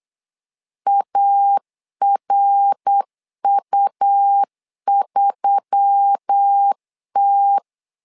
Morse code, heard as a keyed tone, about 8 wpm, 800 Hz. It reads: ARU3T
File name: LEZIONI / 4 Q1S3 8wpm 3037